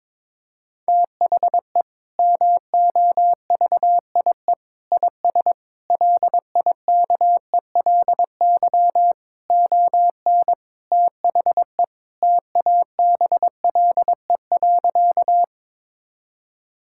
Morse code, heard as a keyed tone, about 22 wpm, 700 Hz. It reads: THE MOVIE IS LIKELY ON THE TABLE.
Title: Sentences from Top 1000 Words - Review Part 5 - 22wpm